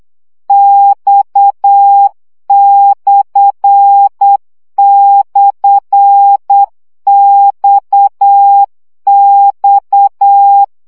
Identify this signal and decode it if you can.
Morse code, heard as a keyed tone, about 8 wpm, 800 Hz. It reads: X//XX
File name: CW gyakorlas x per 7 baud